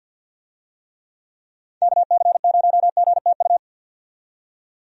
Morse code, heard as a keyed tone, about 47 wpm, 700 Hz. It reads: KC0GTU
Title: USA Call Signs - 6 Characters - 50wpm